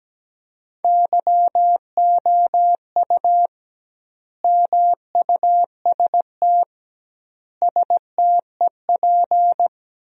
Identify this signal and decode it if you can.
Morse code, heard as a keyed tone, about 17 wpm, 700 Hz. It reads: YOU MUST STEP
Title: 2x Word-Spacing - Sentences from Top 400 Words - 17wpm